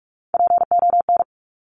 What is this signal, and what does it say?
Morse code, heard as a keyed tone, about 46 wpm, 700 Hz. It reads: PON
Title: Top Three Letter Combinations in Words 45wpm